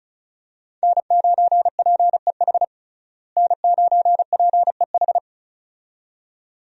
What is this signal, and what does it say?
Morse code, heard as a keyed tone, about 34 wpm, 700 Hz. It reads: N9PEH N9PEH
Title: USA Call Signs - 5 Characters - Repeated 2x - 35wpm